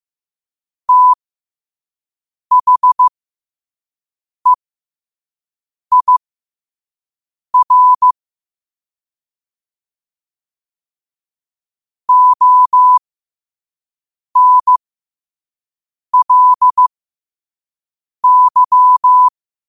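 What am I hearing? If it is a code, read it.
Morse code, heard as a keyed tone, about 15 wpm, spaced as for 5 wpm, 1000 Hz. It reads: THEIR ONLY
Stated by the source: Morse sent at 5 wpm.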